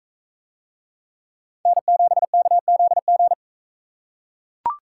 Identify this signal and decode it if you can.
Morse code, heard as a keyed tone, about 40 wpm, 700 Hz. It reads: N7KZG